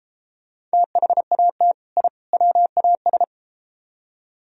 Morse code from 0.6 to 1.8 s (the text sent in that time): THAT